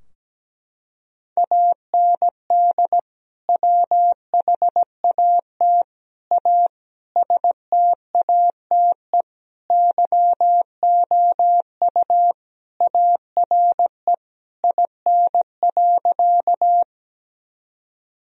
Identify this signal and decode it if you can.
Morse code, heard as a keyed tone, about 17 wpm, 700 Hz. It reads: AND WHAT A STATE YOU ARE IN.
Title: Sentences from Top 1000 Words - Review Part 2 - 17wpm